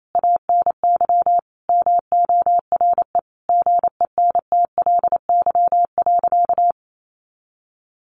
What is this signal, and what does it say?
Morse code, heard as a keyed tone, about 28 wpm, 700 Hz. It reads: ANY MORE GENTLY.
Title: Sentences from Top 1500 Words 28wpm